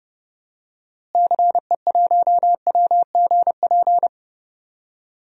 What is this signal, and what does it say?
Morse code, heard as a keyed tone, about 30 wpm, 700 Hz. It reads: CE1WGP
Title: DXCC-100 Call Signs - 6 Characters - Repeated 3x - 30wpm